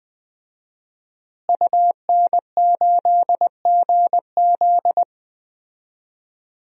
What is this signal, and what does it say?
Morse code, heard as a keyed tone, about 20 wpm, 700 Hz. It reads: UN8GZ
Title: DXCC-100 Call Signs - 5 Characters - Repeated 3x - 20wpm